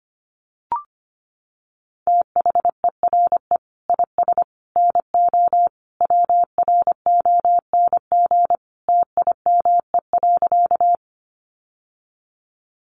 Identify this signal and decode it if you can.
Morse code, heard as a keyed tone, about 25 wpm, 700 Hz. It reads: THERE IS NO WRONG TIME.
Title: Sentences from Top 900 Words 25wpm